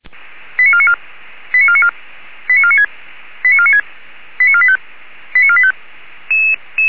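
Five-tone selective call (ZVEI), heard as quick runs of five tones, tones of about 70 ms. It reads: 89494 89494 89496 89496 89485 89485
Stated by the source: ZVEI five-tone selective call 89155 received by SDR.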